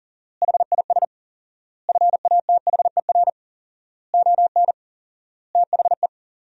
Morse code, heard as a keyed tone, about 40 wpm, 700 Hz. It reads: HIS FATHER ON THE
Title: 4x Word-Spacing - Sentences from Top 500 Words - 40wpm